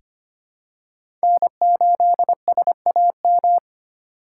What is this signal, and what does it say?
Morse code, heard as a keyed tone, about 25 wpm, 700 Hz. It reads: N8SAM
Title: USA Call Signs - 5 Characters - 4x Character Spacing - 25wpm